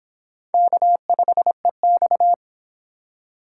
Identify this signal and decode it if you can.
Morse code, heard as a keyed tone, about 25 wpm, 700 Hz. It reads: K5EX